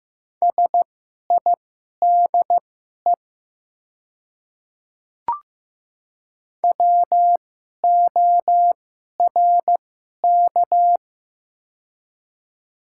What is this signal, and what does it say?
Morse code, heard as a keyed tone, about 15 wpm, spaced as for 11 wpm, 700 Hz. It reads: SIDE WORK
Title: 2x Character Spacing - Top 300 Words - 15wpm